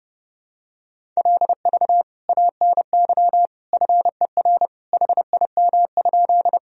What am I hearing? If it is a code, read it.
Morse code, heard as a keyed tone, about 28 wpm, 700 Hz. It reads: LV ANY FER HIM?
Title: Sentences from Top 400 Words and Abbreviations 30wpm